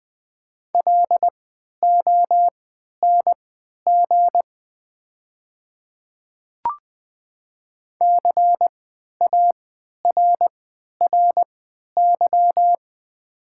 Morse code, heard as a keyed tone, about 20 wpm, spaced as for 11 wpm, 690 Hz. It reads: LONG CARRY
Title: 3x Character Spacing - Top 300 Words - 20wpm